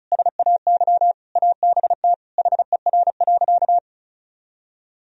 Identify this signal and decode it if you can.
Morse code, heard as a keyed tone, about 36 wpm, 700 Hz. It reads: SAY ABT HER.
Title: Sentences from Top 200 Words and Abbreviations 35wpm